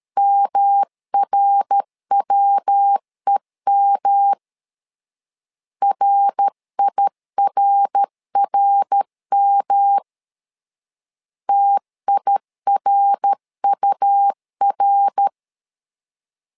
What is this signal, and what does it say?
Morse code, heard as a keyed tone, about 13 wpm, 800 Hz. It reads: MRWEM RIRRM TIRUR
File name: LEZIONI / 3 MWRO / 3 MWR 12wpm 1711